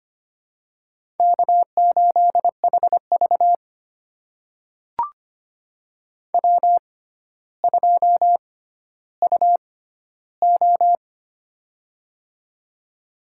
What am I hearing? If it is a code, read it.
Morse code, heard as a keyed tone, about 25 wpm, spaced as for 13 wpm, 700 Hz. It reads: K8HV W2UO
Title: USA Call Signs - 4 Characters - 6x Character Spacing - 25wpm